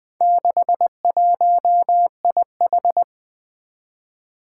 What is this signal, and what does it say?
Morse code, heard as a keyed tone, about 20 wpm, 700 Hz. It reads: 61IH
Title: Four Challenging Letter-Number Combinations 20wpm